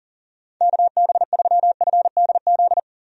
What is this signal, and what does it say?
Morse code, heard as a keyed tone, about 40 wpm, 700 Hz. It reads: KB3FDZ
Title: USA Call Signs - 6 Characters - 40wpm